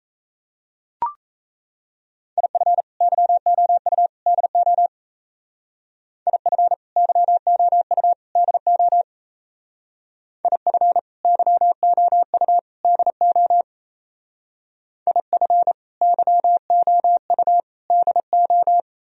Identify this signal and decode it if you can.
Morse code, heard as a keyed tone, about 34 wpm, 700 Hz. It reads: IF YOU DO IF YOU DO IF YOU DO IF YOU DO
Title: Speed-Racing - Sets of 3 Words 28wpm